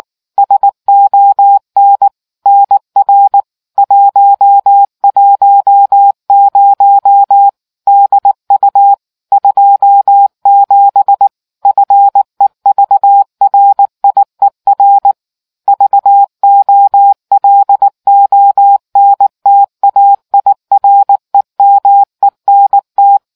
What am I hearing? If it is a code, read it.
Morse code, heard as a keyed tone, about 19 wpm, 800 Hz. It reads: SON NR 110 DU 27 FEVRIER VOLONTAIREMENT